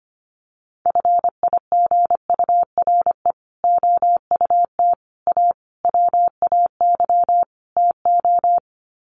Morse code, heard as a keyed tone, about 25 wpm, 700 Hz. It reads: FIGURE OUT A WAY TO
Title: Sets of 5 Words Encore2 25wpm